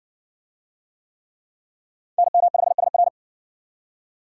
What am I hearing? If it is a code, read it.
Morse code, heard as a keyed tone, about 52 wpm, 700 Hz. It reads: NM5SR